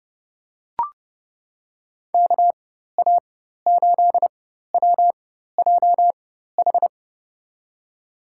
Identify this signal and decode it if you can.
Morse code, heard as a keyed tone, about 30 wpm, spaced as for 14 wpm, 700 Hz. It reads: KA8WJH
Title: USA Call Signs - 4x Character Spacing - 30wpm